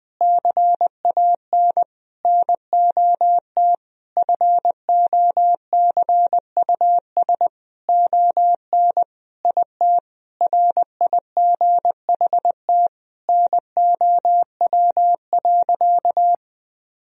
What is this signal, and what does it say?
Morse code, heard as a keyed tone, about 21 wpm, 700 Hz. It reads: CAN NOT FOCUS ON IT RIGHT NOW.
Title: Sentences from Top 1000 Words - Review Part 4 - 20wpm